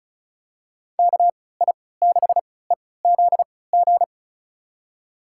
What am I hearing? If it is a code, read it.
Morse code, heard as a keyed tone, about 36 wpm, spaced as for 20 wpm, 700 Hz. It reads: KI6EZG
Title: USA Call Signs - 6 Characters - 3x Character Spacing - 35wpm